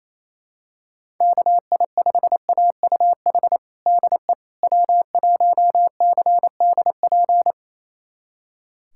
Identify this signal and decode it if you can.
Morse code, heard as a keyed tone, about 27 wpm, 700 Hz. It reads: KI5AUH DE W1CDP
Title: USA Callsign DE Callsign 28wpm